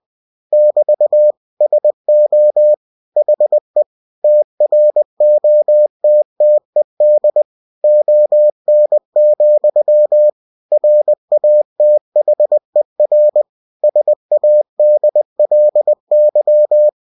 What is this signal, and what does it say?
Morse code, heard as a keyed tone, about 20 wpm, 600 Hz. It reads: <BT> SO HE TROTTED ON, RATHER SADLY
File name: cw-books / winnie-the-pooh / winnie-20wpm-page-0053